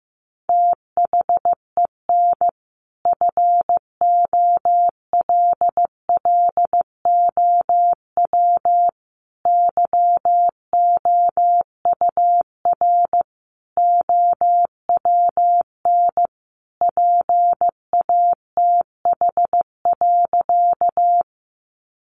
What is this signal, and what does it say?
Morse code, heard as a keyed tone, about 15 wpm, 700 Hz. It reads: THEN FOLLOW YOUR OWN PATH.
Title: Sentences from Top 1400 Words 15wpm